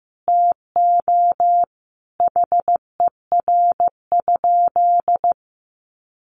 Morse code, heard as a keyed tone, about 14 wpm, 700 Hz. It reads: TO HER?
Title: Sentences from Top 200 Words 15wpm